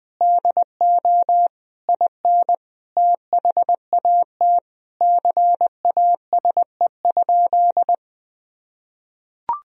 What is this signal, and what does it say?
Morse code, heard as a keyed tone, about 20 wpm, 700 Hz. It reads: DO IN THAT CASE?
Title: Sentences from Top 1000 Words - Review Part 3 - 20wpm